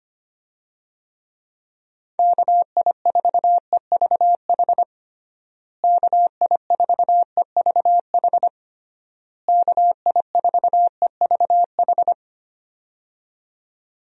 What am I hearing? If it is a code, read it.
Morse code, heard as a keyed tone, about 25 wpm, 700 Hz. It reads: KI4EVH KI4EVH KI4EVH
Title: USA Call Signs - 6 Characters - Repeated 3x - 25wpm